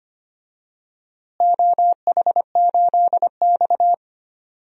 Morse code, heard as a keyed tone, about 25 wpm, 700 Hz. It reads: OH8X
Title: DXCC-100 Call Signs - 4 Characters - Repeated 2x - 25wpm